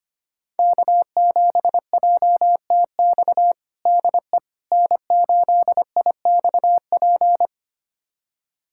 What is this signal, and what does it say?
Morse code, heard as a keyed tone, about 25 wpm, 700 Hz. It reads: K7JTX DE N8IXP
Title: USA Callsign DE Callsign - 5 Characters - 25wpm